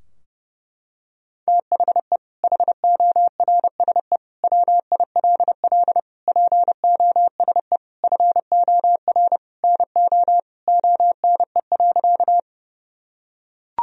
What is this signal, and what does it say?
Morse code, heard as a keyed tone, about 30 wpm, 700 Hz. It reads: THE HORSE WILL POSE FOR NO ONE.
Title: Sentences from Top 1000 Words - Review Part 2 - 30wpm